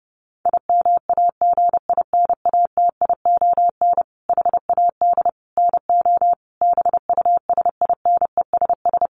Morse code, heard as a keyed tone, about 30 wpm, 700 Hz. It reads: IMAGINATION HAD NO BUSINESS
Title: Sentences from Top 1300 Words 30wpm